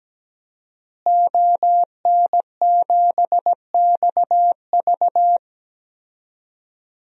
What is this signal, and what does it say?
Morse code, heard as a keyed tone, about 17 wpm, 700 Hz. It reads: ON7XV
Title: DXCC-100 Call Signs - 5 Characters - Repeated 2x - 17wpm